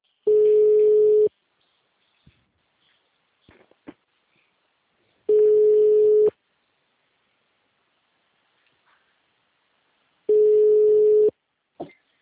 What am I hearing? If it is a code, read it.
Ringback tone (Europe, 425 Hz).